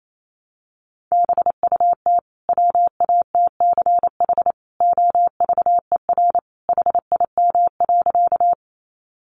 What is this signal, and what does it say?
Morse code, heard as a keyed tone, about 28 wpm, 700 Hz. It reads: BUT WATCH OVER HIM.
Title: Sentences from Top 300 Words 28wpm